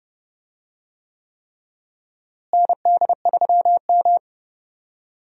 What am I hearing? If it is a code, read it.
Morse code, heard as a keyed tone, about 30 wpm, 700 Hz. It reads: ND3M
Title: USA Call Signs - 4 Characters - Repeated 3x - 30wpm